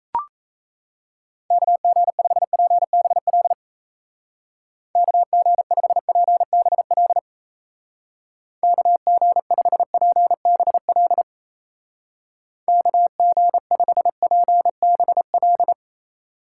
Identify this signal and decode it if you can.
Morse code, heard as a keyed tone, about 37 wpm, 700 Hz. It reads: KG5PBL KG5PBL KG5PBL KG5PBL